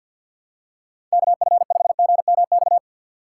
Morse code, heard as a keyed tone, about 47 wpm, 700 Hz. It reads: KF5GMX
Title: USA Call Signs - 6x Character Spacing - 50wpm